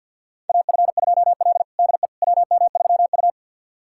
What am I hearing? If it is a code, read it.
Morse code, heard as a keyed tone, about 50 wpm, 690 Hz. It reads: AF2L DE WM3U
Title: USA Callsign DE Callsign - 4 Characters - 50wpm